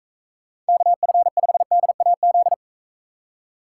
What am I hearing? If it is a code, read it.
Morse code, heard as a keyed tone, about 42 wpm, 700 Hz. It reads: KF5DAZ